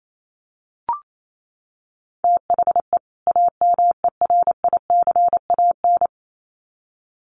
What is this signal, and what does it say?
Morse code, heard as a keyed tone, about 28 wpm, 700 Hz. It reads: THE AMERICAN
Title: Sets of 2 Words 28wpm